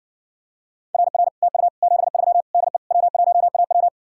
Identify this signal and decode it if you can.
Morse code, heard as a keyed tone, about 70 wpm, 700 Hz. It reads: RR TU 73 DE W1AW